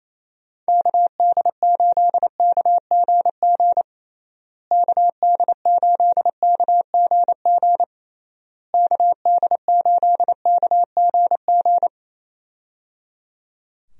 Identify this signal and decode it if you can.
Morse code, heard as a keyed tone, about 28 wpm, 700 Hz. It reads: KD8KGG KD8KGG KD8KGG
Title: USA Call Signs - 6 Characters - Repeated 3x - 28wpm